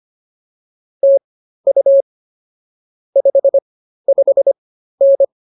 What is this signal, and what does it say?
Morse code, heard as a keyed tone, about 24 wpm, spaced as for 13 wpm, 550 Hz. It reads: TU 55N